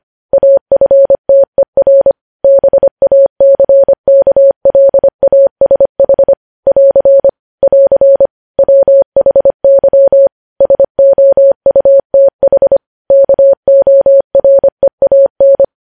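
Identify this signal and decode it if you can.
Morse code, heard as a keyed tone, about 25 wpm, 550 Hz. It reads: AFTER BACKLASH <AR> <AR> WHY SOUTH KOREAN
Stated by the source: Morse code news headlines sent at 25 wpm.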